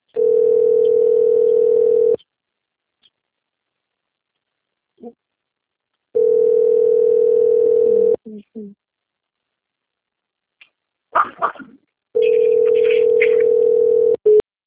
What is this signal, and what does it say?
Ringback tone (North America), 440 + 480 Hz, with speech.
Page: Bauru - SP